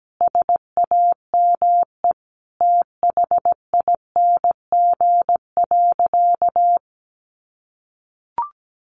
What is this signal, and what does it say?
Morse code, heard as a keyed tone, about 17 wpm, 700 Hz. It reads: SAME THING.
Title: Sentences from Top 1100 Words 17wpm